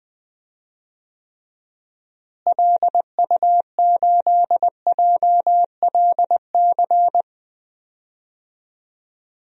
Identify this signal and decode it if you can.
Morse code, heard as a keyed tone, about 20 wpm, 700 Hz. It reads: LU8JLC
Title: DXCC-100 Call Signs - 6 Characters - Repeated 2x - 20wpm